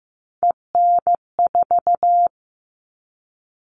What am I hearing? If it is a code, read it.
Morse code, heard as a keyed tone, about 15 wpm, 700 Hz. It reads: EN4
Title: All Cut Numbers - 3-Digit 15wpm